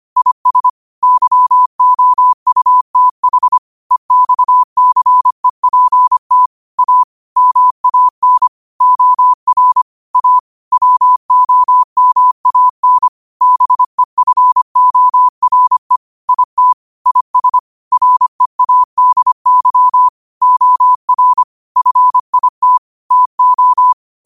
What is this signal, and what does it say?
Morse code, heard as a keyed tone, about 25 wpm, 1000 Hz. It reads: IS YOUTH EXCEPT A MAN OR A WOMAN BEFORE IT IS READY OR FIT TO
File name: qotdfiles / 25wpm